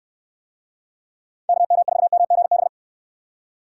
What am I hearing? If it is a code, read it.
Morse code, heard as a keyed tone, about 66 wpm, 700 Hz. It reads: KM4MGB